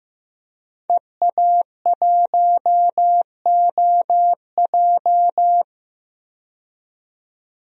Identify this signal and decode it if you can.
Morse code, heard as a keyed tone, about 15 wpm, 700 Hz. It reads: EA1OJ
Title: DXCC-100 Call Signs - 5 Characters - 15wpm